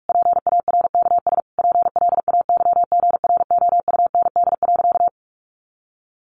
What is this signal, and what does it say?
Morse code, heard as a keyed tone, about 44 wpm, 700 Hz. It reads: PARKS PLAYGROUND.
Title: Additional Easy Phrases 45wpm